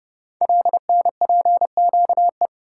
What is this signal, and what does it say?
Morse code, heard as a keyed tone, about 30 wpm, 700 Hz. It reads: LNPQE
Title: Code Groups - 5 Letters - 30wpm